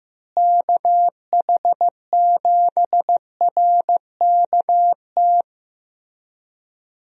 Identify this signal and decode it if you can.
Morse code, heard as a keyed tone, about 15 wpm, 700 Hz. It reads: KH7RKT
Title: USA Call Signs - 6 Characters - Repeated 3x - 15wpm